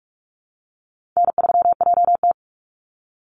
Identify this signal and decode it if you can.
Morse code, heard as a keyed tone, about 46 wpm, 700 Hz. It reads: N3JT
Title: Easy Call Signs Repeated 3x - 45wpm